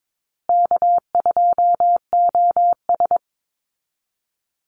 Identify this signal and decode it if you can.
Morse code, heard as a keyed tone, about 22 wpm, 700 Hz. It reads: K2OS
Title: Call Signs Repeated 2x - 22wpm